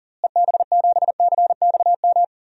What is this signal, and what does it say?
Morse code, heard as a keyed tone, about 40 wpm, 700 Hz. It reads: EB7CXM
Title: DXCC-100 Call Signs - 6 Characters - Repeated 2x - 40wpm